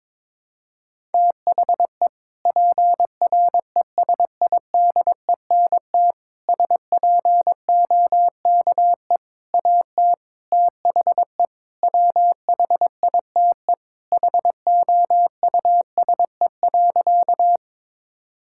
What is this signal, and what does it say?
Morse code, heard as a keyed tone, about 22 wpm, 700 Hz. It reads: THE PRESIDENT SPOKE AT THE WHITE HOUSE.
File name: practice / 22wpm / Sentences with 4-Word Phrases - 22wpm